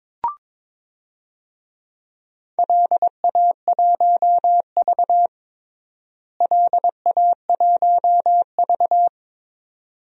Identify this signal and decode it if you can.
Morse code, heard as a keyed tone, about 22 wpm, 700 Hz. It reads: LA1V LA1V
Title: DXCC-100 Call Signs - 4 Characters - Repeated 2x - 22wpm